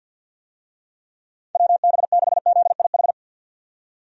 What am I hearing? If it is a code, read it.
Morse code, heard as a keyed tone, about 52 wpm, 700 Hz. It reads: WB6ZIH
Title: USA Call Signs - 6 Characters - 50wpm